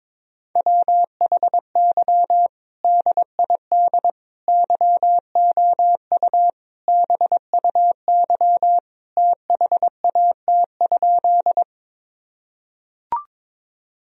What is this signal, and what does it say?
Morse code, heard as a keyed tone, about 22 wpm, 700 Hz. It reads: WHY DID YOU BUY THAT?
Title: Sentences from Top 1000 Words - Review Part 4 - 22wpm